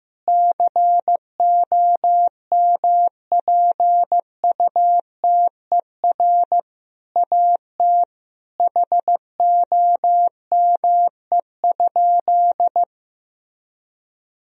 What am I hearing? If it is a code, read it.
Morse code, heard as a keyed tone, about 16 wpm, 700 Hz. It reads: COMPUTER AT HOME?
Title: Sentences from Top 1000 Words - Review Part 3 - 15wpm